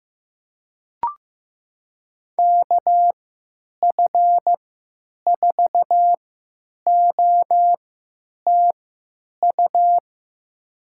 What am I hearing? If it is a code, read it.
Morse code, heard as a keyed tone, about 15 wpm, spaced as for 9 wpm, 700 Hz. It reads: KF4OTU
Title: USA Call Signs - 6 Characters - 3x Character Spacing - 15wpm